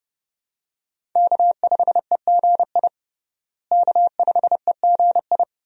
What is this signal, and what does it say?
Morse code, heard as a keyed tone, about 30 wpm, 700 Hz. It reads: K5EGI K5EGI
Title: USA Call Signs - 5 Characters - Repeated 2x - 30wpm